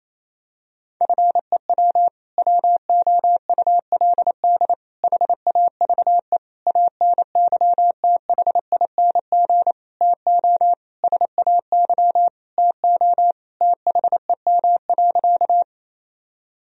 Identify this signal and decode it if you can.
Morse code, heard as a keyed tone, about 28 wpm, 700 Hz. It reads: FEW WOULD HAVE ANYTHING TO SAY TO THEM.
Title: Sentences from Top 1000 Words - Review Part 1 - 28wpm